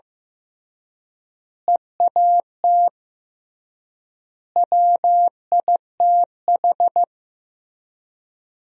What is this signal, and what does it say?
Morse code, heard as a keyed tone, about 16 wpm, 700 Hz. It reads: EAT WITH
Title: 3x Word-Spacing - Sentences from Top 300 Words - 15wpm